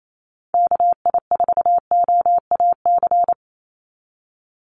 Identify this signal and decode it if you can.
Morse code, heard as a keyed tone, about 28 wpm, 700 Hz. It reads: KI4OAC
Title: USA Call Signs 28wpm